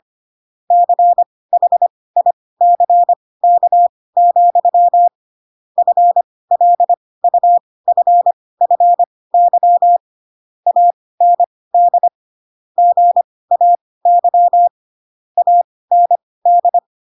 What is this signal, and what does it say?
Morse code, heard as a keyed tone, about 25 wpm, spaced as for 18 wpm, 700 Hz. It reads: CHICK, FLUFFY AND GAY AND
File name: Pinocchio / Pinocchio ch5 12 25 ik1hge